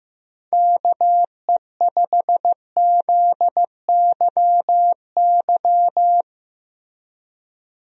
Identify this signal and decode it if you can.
Morse code, heard as a keyed tone, about 16 wpm, 700 Hz. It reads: KE5ZYY
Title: USA Call Signs - 6 Characters - Repeated 2x - 15wpm